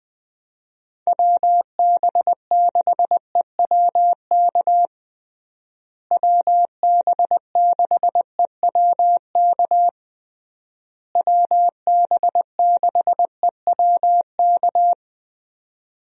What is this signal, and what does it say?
Morse code, heard as a keyed tone, about 21 wpm, 700 Hz. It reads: WB6EWK WB6EWK WB6EWK
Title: USA Call Signs - 6 Characters - Repeated 3x - 20wpm